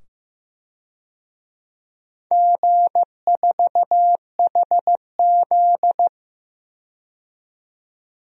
Morse code, heard as a keyed tone, about 15 wpm, 700 Hz. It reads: G4HZ
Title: DXCC-100 Call Signs - 4 Characters - 15wpm